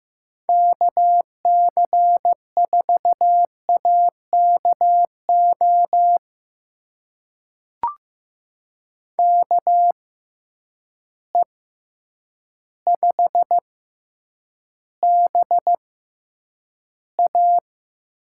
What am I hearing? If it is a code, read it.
Morse code, heard as a keyed tone, about 15 wpm, spaced as for 8 wpm, 700 Hz. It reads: KC4AKO KE5BA
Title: USA Call Signs - 6 Characters - 6x Character Spacing - 15wpm